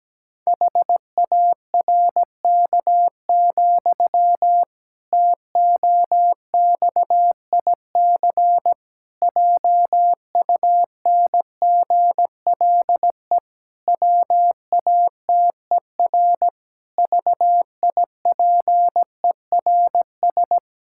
Morse code, heard as a keyed tone, about 17 wpm, 700 Hz. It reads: HARK, TOXIC JUNGLE WATER VIPERS
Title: Pangram Challenge v2 - 17wpm